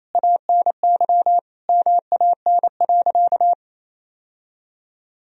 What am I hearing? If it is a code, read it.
Morse code, heard as a keyed tone, about 28 wpm, 700 Hz. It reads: ANY MAN.
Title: Sentences from Top 1000 Words - Review Part 1 - 28wpm